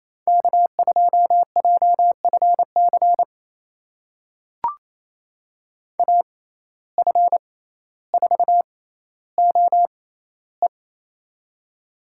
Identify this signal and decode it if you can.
Morse code, heard as a keyed tone, about 28 wpm, spaced as for 14 wpm, 700 Hz. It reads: K2JFC AF4OE
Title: USA Call Signs - 5 Characters - 6x Character Spacing - 28wpm